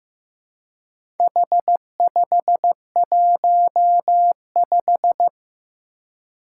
Morse code, heard as a keyed tone, about 15 wpm, 700 Hz. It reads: H515
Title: Four Challenging Letter-Number Combinations 15wpm